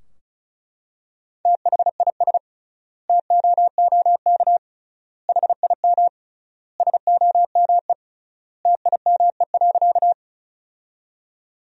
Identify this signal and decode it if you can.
Morse code, heard as a keyed tone, about 34 wpm, 700 Hz. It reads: THIS TOOK HIM SOME TIME.
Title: 3x Word-Spacing - Sentences from Top 300 Words - 35wpm